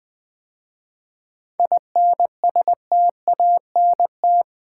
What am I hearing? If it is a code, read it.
Morse code, heard as a keyed tone, about 20 wpm, 700 Hz. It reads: INSTANT
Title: 7 Letter Words - First 3 Letters Given - 20wpm